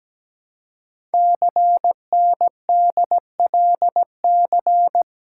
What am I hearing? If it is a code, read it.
Morse code, heard as a keyed tone, about 17 wpm, 700 Hz. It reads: CNDLC
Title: Code Groups - 5 Letters - 17wpm